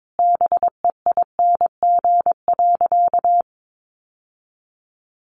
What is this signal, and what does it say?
Morse code, heard as a keyed tone, about 22 wpm, 700 Hz. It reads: BEING.
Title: Sentences from Top 1000 Words 22wpm